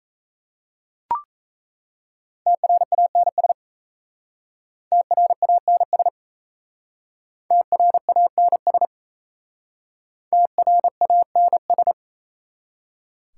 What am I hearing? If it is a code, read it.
Morse code, heard as a keyed tone, about 35 wpm, 700 Hz. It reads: TRANS TRANS TRANS TRANS